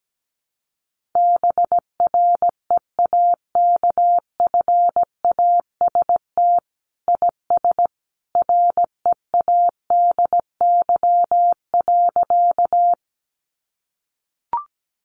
Morse code, heard as a keyed tone, about 17 wpm, 700 Hz. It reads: BREAKFAST IS READY.
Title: Sentences from Top 1500 Words 17wpm